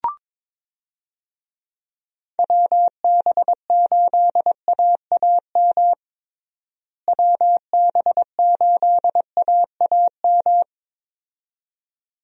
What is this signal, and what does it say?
Morse code, heard as a keyed tone, about 22 wpm, 700 Hz. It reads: WB8AAM WB8AAM
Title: USA Call Signs - 6 Characters - Repeated 2x - 22wpm